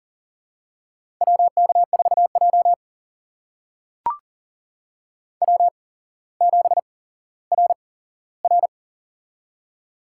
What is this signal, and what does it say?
Morse code, heard as a keyed tone, about 40 wpm, spaced as for 17 wpm, 700 Hz. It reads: WK4J W7RR
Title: USA Call Signs - 4 Characters - 8x Character Spacing - 40wpm